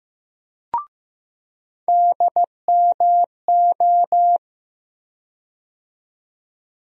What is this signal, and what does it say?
Morse code, heard as a keyed tone, about 15 wpm, 700 Hz. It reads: DMO
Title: USA Call Sign Postfixes - 3 Letters 15wpm